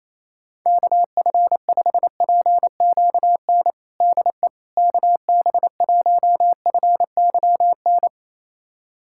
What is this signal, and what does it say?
Morse code, heard as a keyed tone, about 28 wpm, 700 Hz. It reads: KF5PQN DE KB1FYN
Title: USA Callsign DE Callsign - 6 Characters - 28wpm